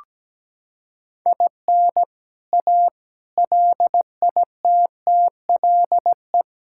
Morse code, heard as a keyed tone, about 17 wpm, 700 Hz. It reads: IN A LITTLE
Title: Sentences from Top 1000 Words - Review Part 5 - 17wpm